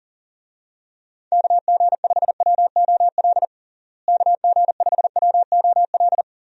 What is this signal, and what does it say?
Morse code, heard as a keyed tone, about 40 wpm, 700 Hz. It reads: KG5WOL KG5WOL
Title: USA Call Signs - 6 Characters - Repeated 3x - 40wpm